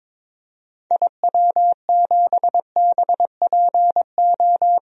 Morse code, heard as a keyed tone, about 21 wpm, 700 Hz. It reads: IW7BPO
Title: DXCC-100 Call Signs - 6 Characters - 22wpm